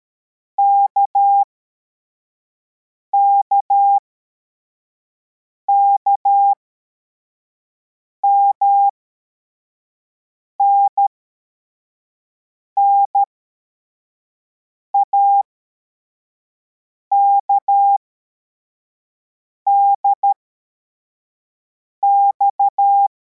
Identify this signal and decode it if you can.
Morse code, heard as a keyed tone, about 13 wpm, spaced as for 4 wpm, 800 Hz. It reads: KKKMN NAKDX
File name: formation / cw / Text24